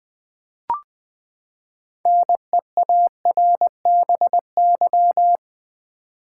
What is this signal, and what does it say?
Morse code, heard as a keyed tone, about 20 wpm, 700 Hz. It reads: NEARBY
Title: Six Letter Words - 20wpm